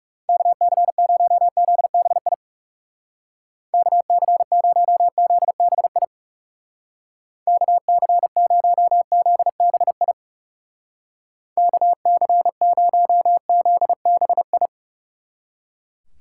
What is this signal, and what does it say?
Morse code, heard as a keyed tone, about 38 wpm, 700 Hz. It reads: KC0ZBI KC0ZBI KC0ZBI KC0ZBI